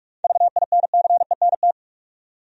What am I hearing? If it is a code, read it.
Morse code, heard as a keyed tone, about 46 wpm, 700 Hz. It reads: VINCENT
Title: USA Common Names - 45wpm